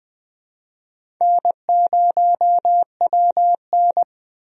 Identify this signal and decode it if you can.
Morse code, heard as a keyed tone, about 20 wpm, 700 Hz. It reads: N0WN
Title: USA Call Signs - 4 Characters - 20wpm